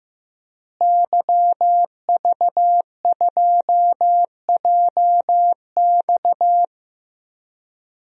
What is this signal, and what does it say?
Morse code, heard as a keyed tone, about 15 wpm, 700 Hz. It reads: YV2JX
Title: DXCC-100 Call Signs - 5 Characters - Repeated 3x - 15wpm